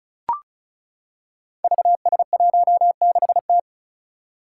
Morse code, heard as a keyed tone, about 36 wpm, 700 Hz. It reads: VS16T
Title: Five Challenging Letter-Number Combinations 35wpm